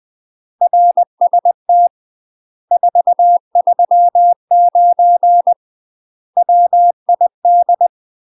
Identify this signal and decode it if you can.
Morse code, heard as a keyed tone, about 20 wpm, 700 Hz. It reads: RST 439 WID
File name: UK MASTER QSOs TRAINING / 20WPM / UK MASTER QSOs 0050